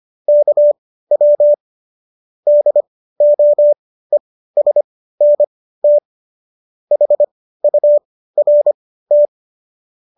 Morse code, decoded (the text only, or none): KW DOESNT HURT